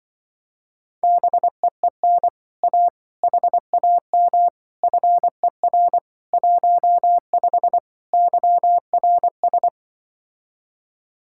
Morse code, heard as a keyed tone, about 24 wpm, 700 Hz. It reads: BEEN A HAM FER 15 YRS
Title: Speed-Racing Common QSO Phrases - 20wpm